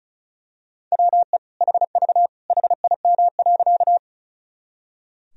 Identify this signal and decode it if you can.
Morse code, heard as a keyed tone, about 36 wpm, 700 Hz. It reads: WE HV HIM.
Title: Sentences from Top 100 Words and Abbreviations 35wpm